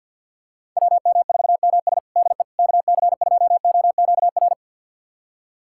Morse code, heard as a keyed tone, about 52 wpm, 700 Hz. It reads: WM4MS DE KC2OQR
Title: USA Callsign DE Callsign 50wpm